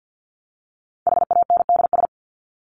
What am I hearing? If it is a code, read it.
Morse code, heard as a keyed tone, about 50 wpm, 700 Hz. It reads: HANDS
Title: ES:S Suffix - Top 100 Words with Base Words 50wpm